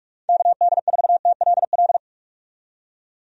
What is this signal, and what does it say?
Morse code, heard as a keyed tone, about 46 wpm, 700 Hz. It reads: KD4TLL
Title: USA Call Signs - 6 Characters - 45wpm